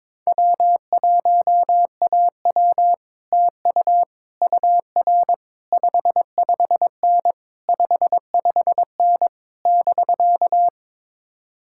Morse code, heard as a keyed tone, about 22 wpm, 700 Hz. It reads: W1AW TU UR 55N 55N <BK>
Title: Speed-Racing Common QSO Phrases - 22wpm